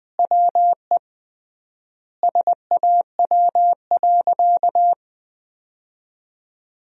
Morse code, heard as a keyed tone, about 20 wpm, 700 Hz. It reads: WE SAW.
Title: 3x Word-Spacing - Sentences from Top 300 Words - 20wpm